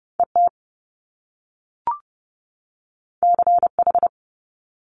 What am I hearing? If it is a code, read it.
Morse code, heard as a keyed tone, about 30 wpm, 700 Hz. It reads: ET CH
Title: Top Two Letter Combinations in Words 30wpm